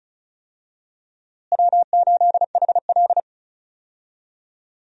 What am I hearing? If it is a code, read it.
Morse code, heard as a keyed tone, about 36 wpm, 700 Hz. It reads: W8HL
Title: USA Call Signs - 4 Characters - Repeated 2x - 35wpm